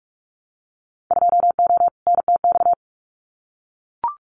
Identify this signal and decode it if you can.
Morse code, heard as a keyed tone, about 44 wpm, 700 Hz. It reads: 2O NTX
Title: Winter Field Day Exchanges Encore 45wpm